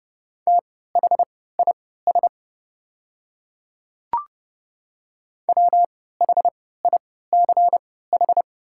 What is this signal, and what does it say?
Morse code, heard as a keyed tone, about 30 wpm, spaced as for 17 wpm, 700 Hz. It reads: THIS WHICH
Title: 3x Character Spacing - Top 300 Words - 30wpm